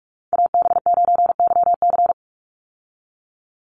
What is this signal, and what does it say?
Morse code, heard as a keyed tone, about 44 wpm, 700 Hz. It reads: AB9YC
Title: Call Signs Repeated 2x - 45wpm